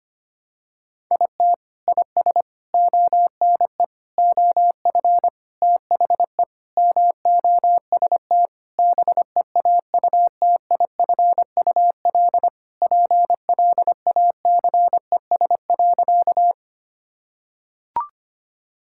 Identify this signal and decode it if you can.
Morse code, heard as a keyed tone, about 25 wpm, 700 Hz. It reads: IT IS ONE OF THE MOST BEAUTIFUL PLACES.
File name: practice / 25wpm / Sentences with 3-Word Phrases - 25wpm